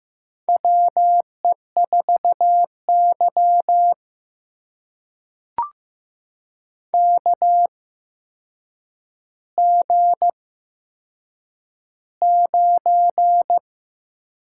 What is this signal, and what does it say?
Morse code, heard as a keyed tone, about 15 wpm, spaced as for 7 wpm, 700 Hz. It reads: WE4Y KG9
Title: USA Call Signs - 4 Characters - 8x Character Spacing - 15wpm